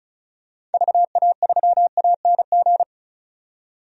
Morse code, heard as a keyed tone, about 34 wpm, 700 Hz. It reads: VA3ANG
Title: DXCC-100 Call Signs - 6 Characters - 35wpm